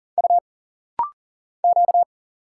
Morse code, heard as a keyed tone, about 40 wpm, 700 Hz. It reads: U Q
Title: Single Letters - Rapid-Fire 40wpm